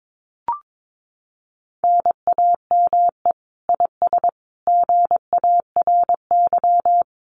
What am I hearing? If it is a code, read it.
Morse code, heard as a keyed tone, about 22 wpm, 700 Hz. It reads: NAME IS GARY
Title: Common QSO Phrases Encore 22wpm